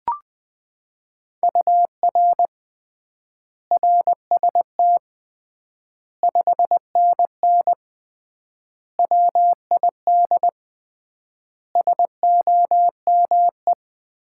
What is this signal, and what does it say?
Morse code, heard as a keyed tone, about 20 wpm, 700 Hz. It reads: UR RST 5NN WID SOME
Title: Common QSO Phrases - 3x Word-Spacing - 20wpm